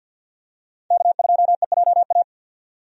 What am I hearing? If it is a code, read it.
Morse code, heard as a keyed tone, about 51 wpm, 700 Hz. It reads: K2EJA
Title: USA Call Signs - 5 Characters - 50wpm